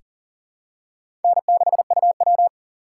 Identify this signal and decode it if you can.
Morse code, heard as a keyed tone, about 37 wpm, 700 Hz. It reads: N6UW
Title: USA Call Signs - 4 Characters - 6x Character Spacing - 40wpm